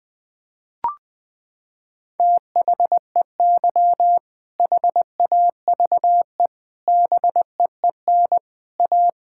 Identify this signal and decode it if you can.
Morse code, heard as a keyed tone, about 21 wpm, 700 Hz. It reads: THEY HAVE BEEN A
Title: Sentences with 3-Word Phrases - 20wpm